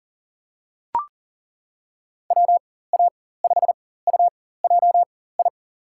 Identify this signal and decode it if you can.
Morse code, heard as a keyed tone, about 40 wpm, spaced as for 19 wpm, 700 Hz. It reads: WA5UJI
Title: USA Call Signs - 6 Characters - 4x Character Spacing - 40wpm